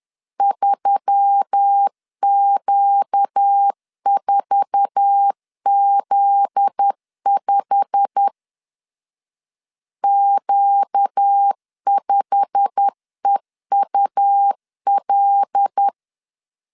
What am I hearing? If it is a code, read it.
Morse code, heard as a keyed tone, about 10 wpm, 800 Hz. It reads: 3Q4Z5 Q5EUL
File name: LEZIONI / 8 YJ45 / 8 YJ45 10wpm 2549